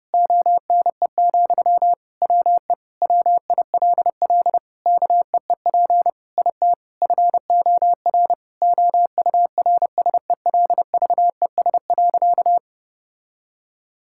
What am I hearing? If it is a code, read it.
Morse code, heard as a keyed tone, about 32 wpm, 700 Hz. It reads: ONE, WE WILL KEEP IT FOR OURSELVES.
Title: Sentences from Top 1600 Words 30wpm